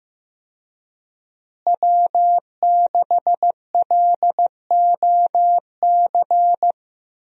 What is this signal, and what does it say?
Morse code, heard as a keyed tone, about 15 wpm, 700 Hz. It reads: W6LOC
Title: USA Call Signs - 5 Characters - Repeated 2x - 15wpm